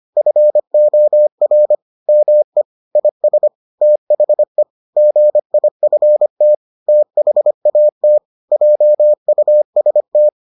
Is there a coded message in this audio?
Yes: Morse code from 0.2 to 10.3 s, heard as a keyed tone, about 25 wpm, 600 Hz. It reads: FOR ME IS THE GIFT THAT JUST